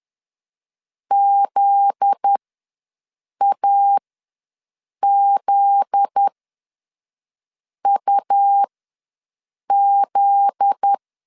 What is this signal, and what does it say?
Morse code, heard as a keyed tone, about 10 wpm, spaced as for 5 wpm, 800 Hz. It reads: ZAZUZ
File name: LEZIONI / 7 GF9Z / 7 GF9Z ws 10wpm 3919